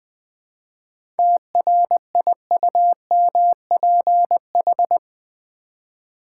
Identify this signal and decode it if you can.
Morse code, heard as a keyed tone, about 20 wpm, 700 Hz. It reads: TRIUMPH
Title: 7 Letter Words - First 3 Letters Given - 20wpm